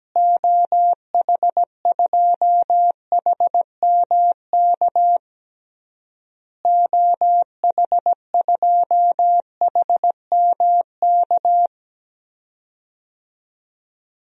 Morse code, heard as a keyed tone, about 17 wpm, 700 Hz. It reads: OH2HMK OH2HMK
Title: DXCC-100 Call Signs - 6 Characters - Repeated 3x - 17wpm